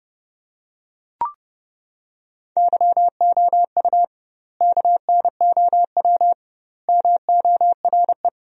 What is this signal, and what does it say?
Morse code, heard as a keyed tone, about 30 wpm, 700 Hz. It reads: YOU KNOW MORE
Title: 2x Word-Spacing - Sentences from Top 100 Words - 30wpm